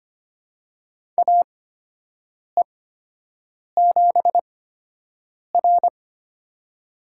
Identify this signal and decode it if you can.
Morse code, heard as a keyed tone, about 25 wpm, spaced as for 7 wpm, 700 Hz. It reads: AE7R
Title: USA Call Signs - 4 Characters - 8x Character Spacing - 25wpm